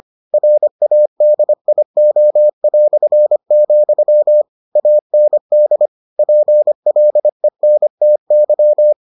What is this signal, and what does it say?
Morse code, heard as a keyed tone, about 25 wpm, 600 Hz. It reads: RADIO", AND PLENTY